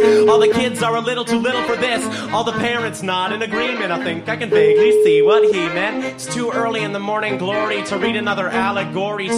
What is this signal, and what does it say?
Ringback tone (Europe, 425 Hz), with speech.